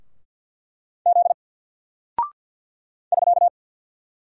Morse code, heard as a keyed tone, about 50 wpm, 700 Hz. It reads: Z <SK>
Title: Single Character (Letter:Number:Punctuation:Prosign) 50wpm